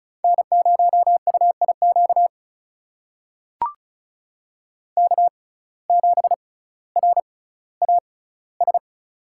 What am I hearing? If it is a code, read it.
Morse code, heard as a keyed tone, about 34 wpm, spaced as for 18 wpm, 700 Hz. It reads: N0UIQ K7RAS
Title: USA Call Signs - 5 Characters - 6x Character Spacing - 35wpm